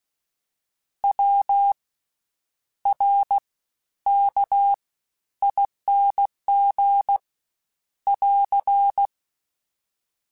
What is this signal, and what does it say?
Morse code, heard as a keyed tone, about 16 wpm, 800 Hz. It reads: W R K ING <AR>
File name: Special INGs-mp3s / Special INGs-mp3s / Fast ing @15